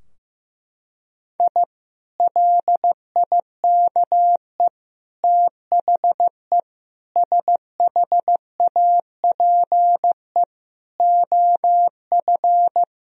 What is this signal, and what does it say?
Morse code, heard as a keyed tone, about 15 wpm, 700 Hz. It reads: I LIKE THE SHAPE OF
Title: Sentences from Top 1000 Words - Review Part 3 - 15wpm